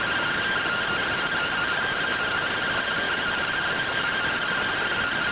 Two-tone teleprinter signal, about 170 Hz shift.